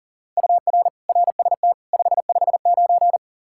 Morse code, heard as a keyed tone, about 37 wpm, 700 Hz. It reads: UR RST 559
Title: Practice QSO - Exchange 1 - Beginning - 40wpm